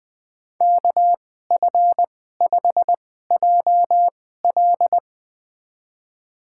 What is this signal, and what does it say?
Morse code, heard as a keyed tone, about 21 wpm, spaced as for 15 wpm, 700 Hz. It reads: KF5JL
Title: USA Call Signs - 5 Characters - 2x Character Spacing - 20wpm